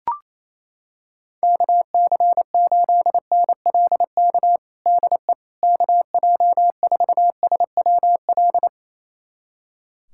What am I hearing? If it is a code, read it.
Morse code, heard as a keyed tone, about 28 wpm, 700 Hz. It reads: KC8NLK DE KJ4SWL
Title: USA Callsign DE Callsign 28wpm